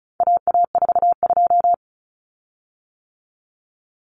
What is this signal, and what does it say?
Morse code, heard as a keyed tone, about 34 wpm, 700 Hz. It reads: AA42
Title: Common Cut Numbers - 4-Digit 35wpm